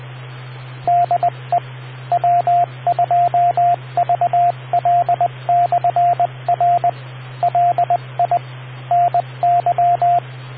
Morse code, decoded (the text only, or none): DE W2VL/R LI NY